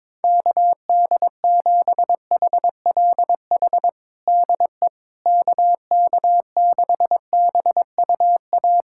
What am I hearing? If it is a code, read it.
Morse code, heard as a keyed tone, about 22 wpm, 700 Hz. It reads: KD7HLH DE KK6BUA
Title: USA Callsign DE Callsign - 6 Characters - 22wpm